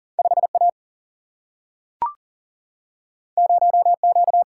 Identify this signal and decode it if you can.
Morse code, heard as a keyed tone, about 40 wpm, 700 Hz. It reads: 5A 0Q
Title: Single-Letter Single-Number 40wpm